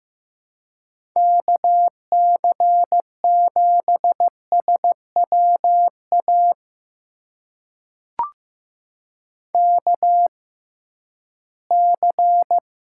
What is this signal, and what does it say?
Morse code, heard as a keyed tone, about 15 wpm, spaced as for 11 wpm, 700 Hz. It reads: KC7SWA KC
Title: USA Call Signs - 6 Characters - 6x Character Spacing - 15wpm